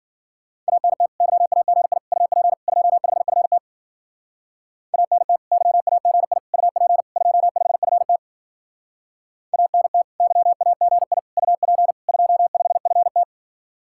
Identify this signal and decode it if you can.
Morse code, heard as a keyed tone, about 52 wpm, 700 Hz. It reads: ANT YAGI UP 25FT ANT YAGI UP 25FT ANT YAGI UP 25FT